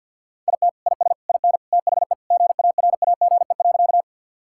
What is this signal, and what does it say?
Morse code, heard as a keyed tone, about 50 wpm, 700 Hz. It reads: IT IS IN THE GARAGE.
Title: Sentences from Top 1600 Words 50wpm